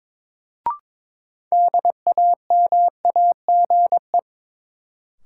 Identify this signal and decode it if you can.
Morse code, heard as a keyed tone, about 22 wpm, 700 Hz. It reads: DAMAGE
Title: Sentences from Top 1100 Words - New Word Drill - 22wpm